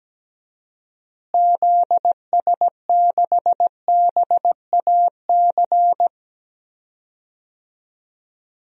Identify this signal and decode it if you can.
Morse code, heard as a keyed tone, about 17 wpm, 700 Hz. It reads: ZS6BAC
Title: DXCC-100 Call Signs - 6 Characters - Repeated 3x - 17wpm